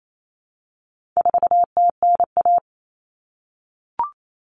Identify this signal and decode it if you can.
Morse code, heard as a keyed tone, about 28 wpm, 700 Hz. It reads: 4TNA
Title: Common Cut Numbers - 4-Digit 28wpm